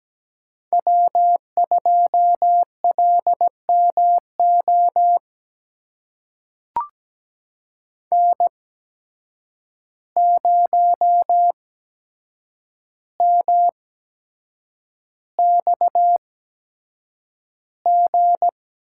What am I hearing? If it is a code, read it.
Morse code, heard as a keyed tone, about 17 wpm, spaced as for 7 wpm, 700 Hz. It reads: W2LMO N0MXG
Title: USA Call Signs - 5 Characters - 8x Character Spacing - 17wpm